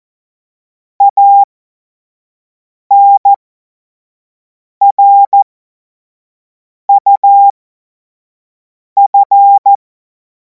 Morse code, heard as a keyed tone, about 14 wpm, spaced as for 5 wpm, 800 Hz. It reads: ANRUF
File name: Morsekurs / Witze / Witz 10. März 2026 25 F70